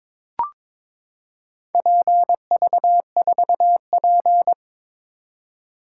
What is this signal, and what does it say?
Morse code, heard as a keyed tone, about 22 wpm, 700 Hz. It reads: PV4P
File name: practice / 22wpm / Four Letter-Number Combinations - 22wpm